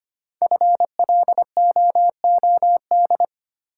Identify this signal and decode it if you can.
Morse code, heard as a keyed tone, about 25 wpm, 700 Hz. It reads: FLOOD
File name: 5 Letter Words - First 3 Letters Given - 25wpm